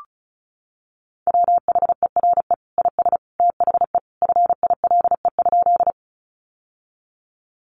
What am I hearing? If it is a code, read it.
Morse code, heard as a keyed tone, about 34 wpm, 700 Hz. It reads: WHERE IS THE FILE?
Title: Sentences from Top 1200 Words 35wpm